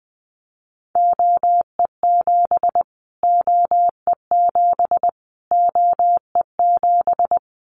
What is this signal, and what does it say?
Morse code, heard as a keyed tone, about 20 wpm, 700 Hz. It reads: OE7 OE7 OE7
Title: Callsign Prefixes for Top 25 DXCC - Repeated 3x - 20wpm